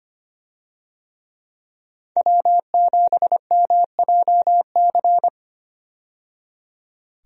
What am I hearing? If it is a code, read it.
Morse code, heard as a keyed tone, about 25 wpm, 690 Hz. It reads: W7MJC
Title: USA Call Signs - 5 Characters - 25wpm